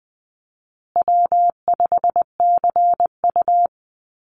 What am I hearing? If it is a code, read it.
Morse code, heard as a keyed tone, about 20 wpm, 700 Hz. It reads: W5CU
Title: Call Signs Repeated 3x - 20wpm